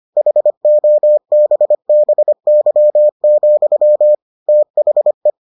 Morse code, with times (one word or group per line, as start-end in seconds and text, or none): 0.2-4.1 HOBBY,
4.5-5.3 THE